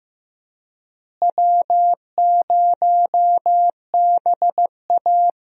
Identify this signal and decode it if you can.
Morse code, heard as a keyed tone, about 15 wpm, 700 Hz. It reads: W0BA
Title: USA Call Signs - 4 Characters - Repeated 2x - 15wpm